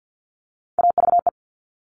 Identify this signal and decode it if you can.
Morse code, heard as a keyed tone, about 50 wpm, 700 Hz. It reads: AVE
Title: All Cut Numbers - 3-Digit 50wpm